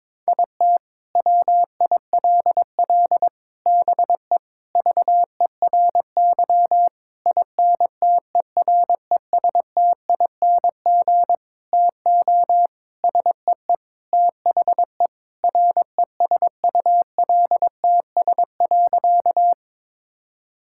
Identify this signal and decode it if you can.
Morse code, heard as a keyed tone, about 22 wpm, 700 Hz. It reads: IT WILL BE VERY INTERESTING TO SEE THE RESULTS.
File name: Sentences with 7-Word Phrases - 22wpm